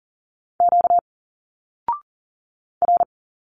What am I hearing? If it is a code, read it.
Morse code, heard as a keyed tone, about 40 wpm, 700 Hz. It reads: Q R
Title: Single Letters 40wpm